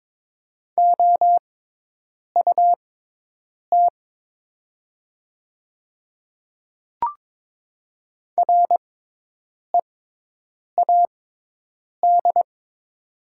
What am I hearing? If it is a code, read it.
Morse code, heard as a keyed tone, about 22 wpm, spaced as for 8 wpm, 700 Hz. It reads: OUT READ
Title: 6x Character Spacing - Top 200 Words - 22wpm